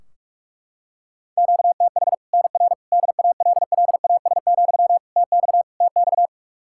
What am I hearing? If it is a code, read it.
Morse code, heard as a keyed tone, about 41 wpm, 700 Hz. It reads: QTH NR DALLAS, TX TX
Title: Practice QSO - Exchange 1 - Beginning - 45wpm